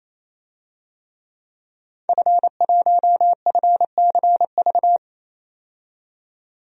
Morse code, heard as a keyed tone, about 28 wpm, 700 Hz. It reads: F1FCV
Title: DXCC-100 Call Signs - 5 Characters - 28wpm